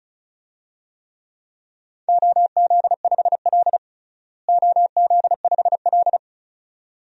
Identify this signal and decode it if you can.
Morse code, heard as a keyed tone, about 34 wpm, 700 Hz. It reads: OZ5L OZ5L
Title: DXCC-100 Call Signs - 4 Characters - Repeated 2x - 35wpm